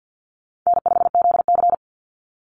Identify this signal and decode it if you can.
Morse code, heard as a keyed tone, about 50 wpm, 700 Hz. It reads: N5ZC
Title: NA SOTA Chaser Callsigns 50wpm